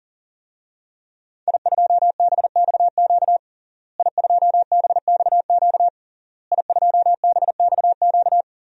Morse code, heard as a keyed tone, about 40 wpm, 700 Hz. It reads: I2BXQ I2BXQ I2BXQ
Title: DXCC-100 Call Signs - 5 Characters - Repeated 3x - 40wpm